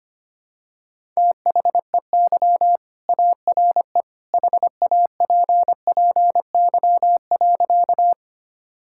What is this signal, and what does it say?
Morse code, heard as a keyed tone, about 25 wpm, 700 Hz. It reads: THEY ARE HAPPY.
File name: practice / 25wpm / Sentences with 2-Word Phrases - 25wpm